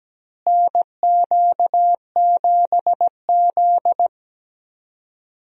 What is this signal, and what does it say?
Morse code, heard as a keyed tone, about 17 wpm, 700 Hz. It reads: NQ7Z
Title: USA Call Signs - 4 Characters - Repeated 3x - 17wpm